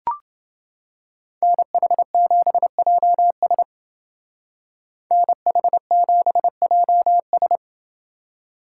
Morse code, heard as a keyed tone, about 29 wpm, 700 Hz. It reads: NH7JS NH7JS